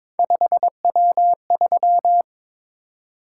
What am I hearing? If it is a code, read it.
Morse code, heard as a keyed tone, about 22 wpm, 700 Hz. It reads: 5W3
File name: practice / 22wpm / Three Letter-Number Combinations - 22wpm